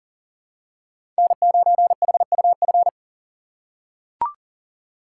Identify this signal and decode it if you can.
Morse code, heard as a keyed tone, about 40 wpm, 700 Hz. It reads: N9HUF
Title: USA Call Signs - 3x Character Spacing - 40wpm